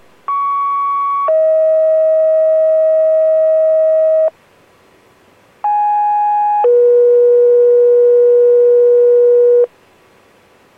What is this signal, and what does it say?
Two-tone page, heard as a one-second tone, then a three-second tone, about 1120 then 640 Hz, about 830 then 480 Hz. It is heard from 0.3 to 9.7 s.